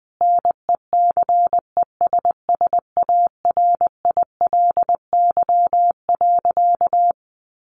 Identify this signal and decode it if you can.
Morse code, heard as a keyed tone, about 20 wpm, 700 Hz. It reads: NECESSARILY.
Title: Sentences from Top 1400 Words 20wpm